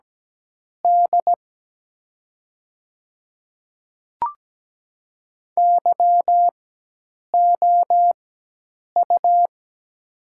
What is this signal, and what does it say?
Morse code, heard as a keyed tone, about 17 wpm, spaced as for 8 wpm, 700 Hz. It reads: D YOU
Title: 4x Character Spacing - Top 300 Words - 17wpm